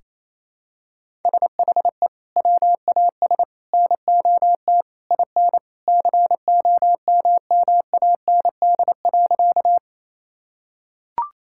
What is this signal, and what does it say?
Morse code, heard as a keyed tone, about 27 wpm, 700 Hz. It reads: SHE WAS NOT IN COMMAND.
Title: Sentences from Top 1600 Words 28wpm